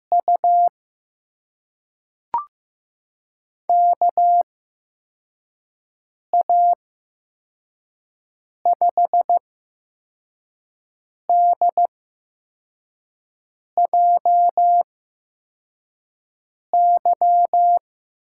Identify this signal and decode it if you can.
Morse code, heard as a keyed tone, about 15 wpm, spaced as for 4 wpm, 700 Hz. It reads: U KA5DJY
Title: USA Call Signs - 6 Characters - 8x Character Spacing - 15wpm